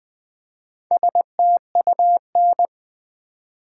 Morse code, heard as a keyed tone, about 20 wpm, 700 Hz. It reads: STUN
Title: Build A Word - 1 Letter At A Time - 4-Letter Words - 20wpm